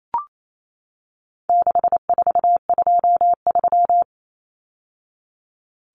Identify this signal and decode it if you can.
Morse code, heard as a keyed tone, about 28 wpm, 700 Hz. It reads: 6423
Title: Numbers - 4-Digit 28wpm